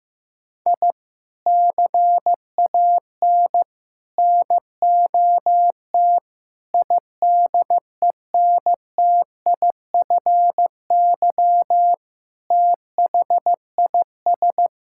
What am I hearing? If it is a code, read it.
Morse code, heard as a keyed tone, about 15 wpm, 700 Hz. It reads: I CAN NOT IDENTIFY THIS
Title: Sentences from Top 1000 Words - Review Part 6 - 15wpm